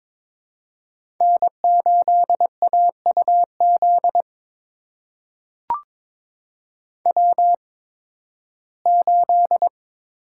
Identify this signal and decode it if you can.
Morse code, heard as a keyed tone, about 23 wpm, spaced as for 15 wpm, 700 Hz. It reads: N8AUZ W8
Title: USA Call Signs - 8x Character Spacing - 22wpm